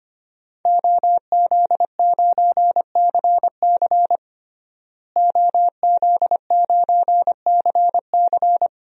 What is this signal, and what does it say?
Morse code, heard as a keyed tone, about 25 wpm, 700 Hz. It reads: OZ9CC OZ9CC
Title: DXCC-100 Call Signs - 5 Characters - Repeated 2x - 25wpm